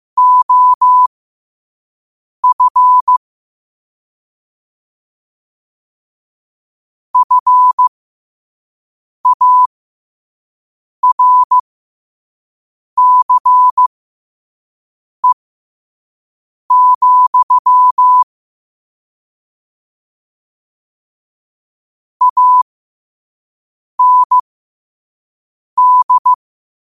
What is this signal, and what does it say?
Morse code, heard as a keyed tone, about 15 wpm, spaced as for 5 wpm, 1000 Hz. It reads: OF FARCE, AND
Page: New quotes every day in morse code at 5 Words per minute.